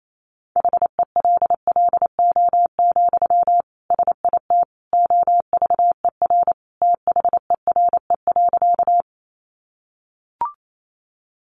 Morse code, heard as a keyed tone, about 28 wpm, 700 Hz. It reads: HELLO, SIT OVER THERE.
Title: Sentences from Top 1100 Words 28wpm